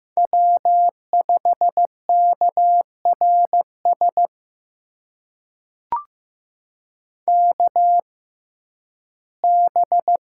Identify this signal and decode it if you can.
Morse code, heard as a keyed tone, about 15 wpm, spaced as for 11 wpm, 700 Hz. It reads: W5KRS KB
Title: USA Call Signs - 6x Character Spacing - 15wpm